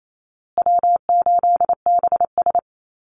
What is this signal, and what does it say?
Morse code, heard as a keyed tone, about 28 wpm, 700 Hz. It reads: W8BS
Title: USA Call Signs Repeated 2x 28wpm